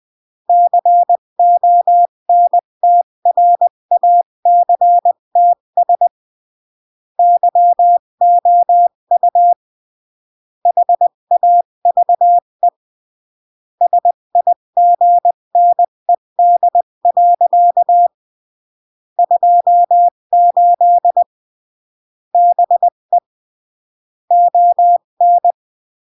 Morse code, heard as a keyed tone, about 20 wpm, 700 Hz. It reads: CONTRACTS YOU HAVE SIGNED. 28 BE ON
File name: CW / jodorowsky / 82 commandments 20-15